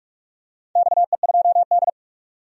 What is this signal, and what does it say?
Morse code, heard as a keyed tone, about 44 wpm, 700 Hz. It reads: XE2D
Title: DXCC-100 Call Signs - 4 Characters - 45wpm